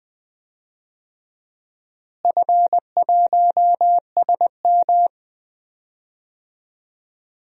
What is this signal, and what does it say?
Morse code, heard as a keyed tone, about 20 wpm, 700 Hz. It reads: F1SM
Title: DXCC-100 Call Signs - 4 Characters - Repeated 2x - 20wpm